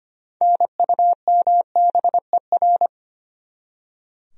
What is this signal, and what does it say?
Morse code, heard as a keyed tone, about 25 wpm, 700 Hz. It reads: NUMBER
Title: Top 100 Words 25wpm